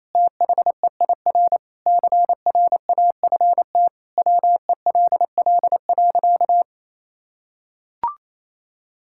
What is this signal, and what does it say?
Morse code, heard as a keyed tone, about 27 wpm, 700 Hz. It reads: THEIR CRAFT WELL.
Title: Sentences from Top 1000 Words - Review Part 6 - 28wpm